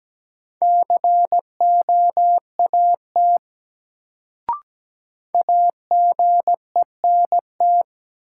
Morse code, heard as a keyed tone, about 17 wpm, 700 Hz. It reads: COAT AGENT
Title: Sentences from Top 1600 Words - New Word Drill Alt - 17wpm